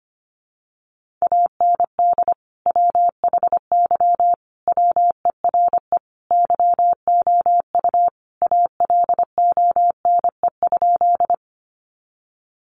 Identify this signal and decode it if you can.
Morse code, heard as a keyed tone, about 25 wpm, 700 Hz. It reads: AND WHY WERE YOU ALONE?
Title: Sentences from Top 1000 Words 25wpm